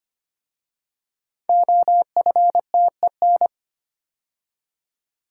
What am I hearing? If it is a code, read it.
Morse code, heard as a keyed tone, about 25 wpm, 700 Hz. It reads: OFTEN
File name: 5 Letter Words - First 3 Letters Given - 25wpm